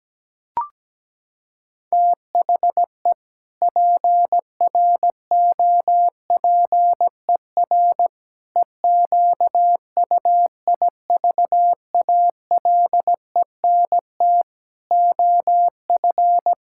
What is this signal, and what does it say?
Morse code, heard as a keyed tone, about 17 wpm, 700 Hz. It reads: THE PROPER EQUIVALENT OF
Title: Sentences from Top 1600 Words 17wpm